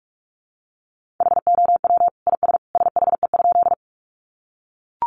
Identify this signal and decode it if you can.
Morse code, heard as a keyed tone, about 44 wpm, 700 Hz. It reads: HOW IS SHE?
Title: Sentences from Top 100 Words 45wpm